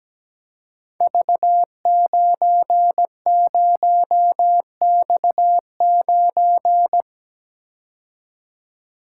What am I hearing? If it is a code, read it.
Morse code, heard as a keyed tone, about 17 wpm, 700 Hz. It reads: V90X9
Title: Code Groups - 5 Letter:Numbers - 17wpm